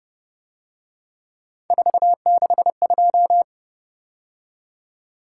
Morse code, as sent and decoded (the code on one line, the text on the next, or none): ....- -.... ..---
462